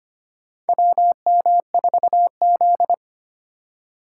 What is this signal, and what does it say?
Morse code, heard as a keyed tone, about 25 wpm, 700 Hz. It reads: WM4Z
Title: USA Call Signs - 4 Characters - 2x Character Spacing - 25wpm